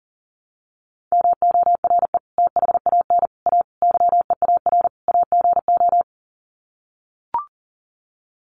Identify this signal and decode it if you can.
Morse code, heard as a keyed tone, about 37 wpm, 700 Hz. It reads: MORE THAN A YEAR AGO
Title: Sets of 5 Words Encore2 40wpm